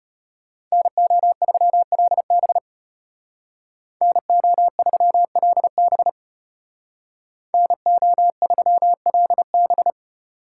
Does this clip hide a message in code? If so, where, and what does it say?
Yes: Morse code from 0.7 to 9.9 s, heard as a keyed tone, about 33 wpm, 700 Hz. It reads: NO3LB NO3LB NO3LB